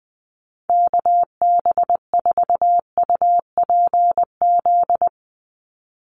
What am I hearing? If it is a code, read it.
Morse code, heard as a keyed tone, about 20 wpm, 700 Hz. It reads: KB4UPZ
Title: USA Call Signs 20wpm